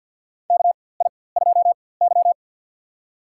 Morse code, heard as a keyed tone, about 47 wpm, spaced as for 23 wpm, 700 Hz. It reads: KI2Y
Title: USA Call Signs - 4x Character Spacing - 50wpm